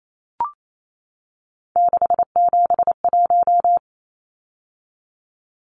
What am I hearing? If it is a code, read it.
Morse code, heard as a keyed tone, about 28 wpm, 700 Hz. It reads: 671
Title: Numbers - 3-Digit 28wpm